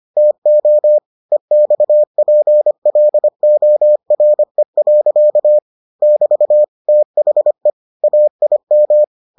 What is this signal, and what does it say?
Morse code, heard as a keyed tone, about 25 wpm, 600 Hz. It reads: TO EXPLORE. <BT> THE AIM